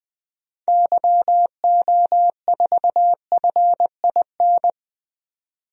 Morse code, heard as a keyed tone, about 21 wpm, 700 Hz. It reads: YO4FIN
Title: DXCC-100 Call Signs - 6 Characters - 20wpm